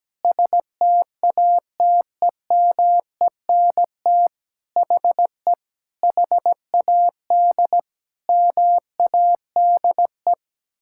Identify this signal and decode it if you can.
Morse code, heard as a keyed tone, about 17 wpm, 700 Hz. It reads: STATEMENT HE HAD MADE
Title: Sentences from Top 1600 Words 17wpm